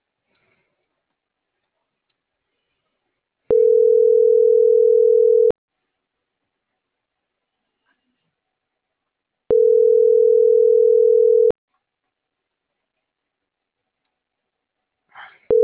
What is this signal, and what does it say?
Ringback tone (North America), 440 + 480 Hz.